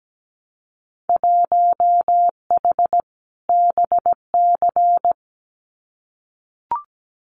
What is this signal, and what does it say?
Morse code, heard as a keyed tone, about 17 wpm, 700 Hz. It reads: 1H BC
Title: Winter Field Day Exchanges 17wpm